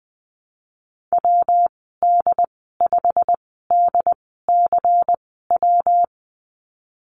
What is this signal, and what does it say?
Morse code, heard as a keyed tone, about 20 wpm, spaced as for 15 wpm, 700 Hz. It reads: WD5DCW
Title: 161 Lesson 26g - Letter g - Call Signs 20wpm